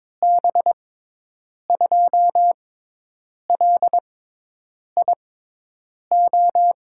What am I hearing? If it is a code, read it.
Morse code, heard as a keyed tone, about 22 wpm, spaced as for 8 wpm, 700 Hz. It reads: B2LIO
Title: USA Call Signs - 6 Characters - 6x Character Spacing - 22wpm